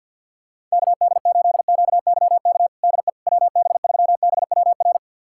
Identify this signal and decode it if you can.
Morse code, heard as a keyed tone, about 51 wpm, 700 Hz. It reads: KD8QYK DE WB3BWR
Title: USA Callsign DE Callsign 50wpm